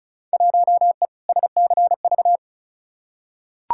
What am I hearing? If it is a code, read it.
Morse code, heard as a keyed tone, about 34 wpm, 700 Hz. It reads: 1E SCV
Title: Field Day Exchanges - 35wpm